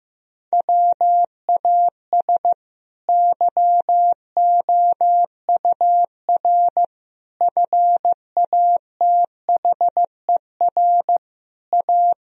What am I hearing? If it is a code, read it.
Morse code, heard as a keyed tone, about 15 wpm, 700 Hz. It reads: WAS YOUR FATHER A
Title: Sentences from Top 1000 Words - Review Part 4 - 15wpm